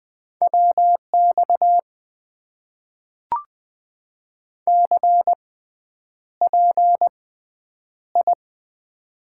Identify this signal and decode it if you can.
Morse code, heard as a keyed tone, about 20 wpm, spaced as for 9 wpm, 700 Hz. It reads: WX CPI
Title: Top 90 QSO Elements - 6x Character Spacing - 20wpm